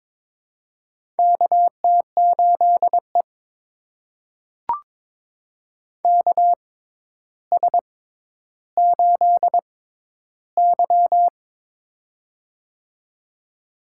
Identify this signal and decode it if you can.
Morse code, heard as a keyed tone, about 22 wpm, spaced as for 11 wpm, 700 Hz. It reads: KT8E KS8Y
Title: USA Call Signs - 4 Characters - 6x Character Spacing - 22wpm